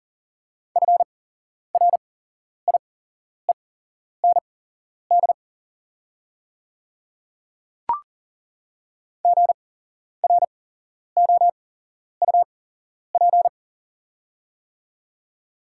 Morse code, heard as a keyed tone, about 40 wpm, spaced as for 11 wpm, 690 Hz. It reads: FRIEND GROUP
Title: 8x Character Spacing - Top 500 Words - 40wpm